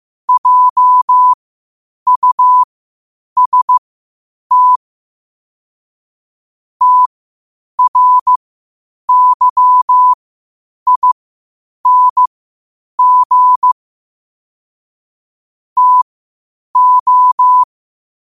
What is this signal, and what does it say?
Morse code, heard as a keyed tone, about 15 wpm, spaced as for 8 wpm, 1000 Hz. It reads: JUST TRYING TO
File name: qotdfiles / 8wpm